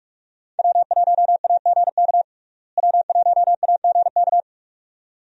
Morse code, heard as a keyed tone, about 46 wpm, 700 Hz. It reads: W1AGK W1AGK
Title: USA Call Signs - 5 Characters - Repeated 2x - 45wpm